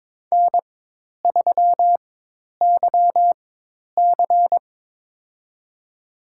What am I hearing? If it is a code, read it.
Morse code, heard as a keyed tone, about 22 wpm, spaced as for 10 wpm, 700 Hz. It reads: N3YC
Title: USA Call Signs - 4x Character Spacing - 22wpm